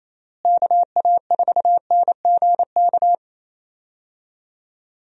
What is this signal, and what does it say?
Morse code, heard as a keyed tone, about 28 wpm, 700 Hz. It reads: KA4NGK
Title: USA Call Signs - 6 Characters - Repeated 3x - 28wpm